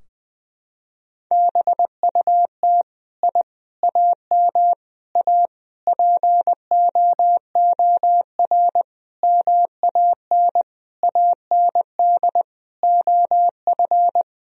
Morse code, heard as a keyed tone, about 20 wpm, 700 Hz. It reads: BUT I AM A POOR MAN AND OF
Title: Sentences from Top 1000 Words - Review Part 3 - 20wpm